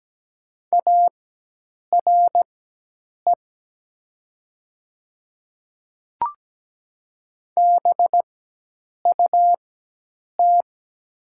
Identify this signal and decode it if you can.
Morse code, heard as a keyed tone, about 17 wpm, spaced as for 8 wpm, 700 Hz. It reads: ARE BUT
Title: 4x Character Spacing - Top 300 Words - 17wpm